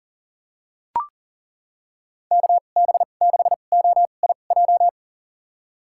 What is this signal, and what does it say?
Morse code, heard as a keyed tone, about 37 wpm, spaced as for 28 wpm, 700 Hz. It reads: KB6OIJ
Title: USA Call Signs - 6 Characters - 2x Character Spacing - 40wpm